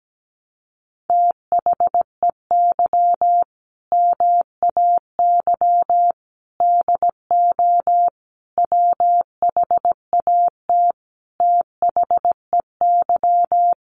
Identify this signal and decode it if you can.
Morse code, heard as a keyed tone, about 17 wpm, 700 Hz. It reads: THEY MAY DO WHAT THEY
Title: Sentences from Top 1200 Words 17wpm